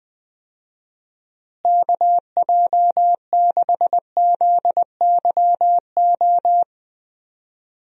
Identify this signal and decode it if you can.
Morse code, heard as a keyed tone, about 21 wpm, 700 Hz. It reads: KJ6ZYO
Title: USA Call Signs - 6 Characters - Repeated 3x - 20wpm